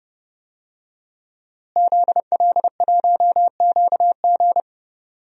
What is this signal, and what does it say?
Morse code, heard as a keyed tone, about 30 wpm, 700 Hz. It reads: ZL1QG
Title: DXCC-100 Call Signs - 5 Characters - Repeated 2x - 30wpm